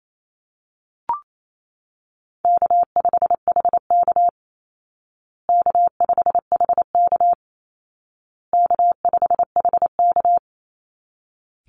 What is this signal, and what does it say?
Morse code, heard as a keyed tone, about 28 wpm, 700 Hz. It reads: K5HK K5HK K5HK
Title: Call Signs Repeated 2x - 28wpm